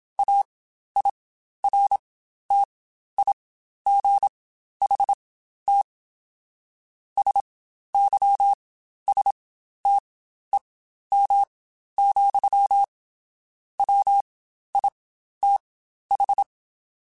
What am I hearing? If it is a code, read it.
Morse code, heard as a keyed tone, about 27 wpm, spaced as for 12 wpm, 800 Hz. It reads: AIRTIGHT SYSTEM, WITH
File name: Quotes Spaced / Quotes Spaced-mp3s / Airtight sp @25